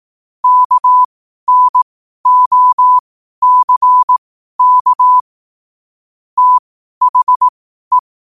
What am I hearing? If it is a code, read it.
Morse code, heard as a keyed tone, about 18 wpm, spaced as for 12 wpm, 1000 Hz. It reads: KNOCK THE
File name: qotdfiles / 12wpm